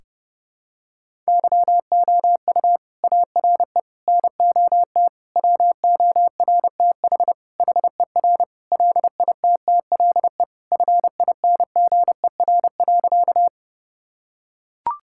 Morse code, heard as a keyed tone, about 30 wpm, 700 Hz. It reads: YOU ARE NOT WORTH HER LITTLE FINGER.
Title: Sentences from Top 1600 Words 30wpm